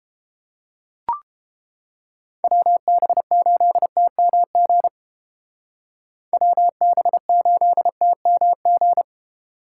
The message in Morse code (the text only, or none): WB8TMG WB8TMG